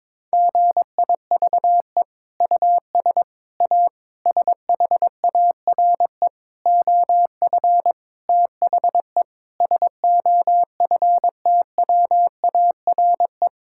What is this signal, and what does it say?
Morse code, heard as a keyed tone, about 23 wpm, 700 Hz. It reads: GIVE US A SHARE OF THE SOFTWARE
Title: Sentences from Top 1000 Words - Review Part 6 - 22wpm